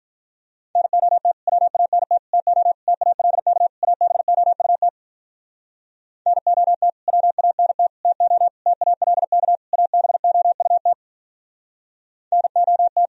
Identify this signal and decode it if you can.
Morse code, heard as a keyed tone, about 46 wpm, 700 Hz. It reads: NOT WANT TO TALK ABOUT NOT WANT TO TALK ABOUT NOT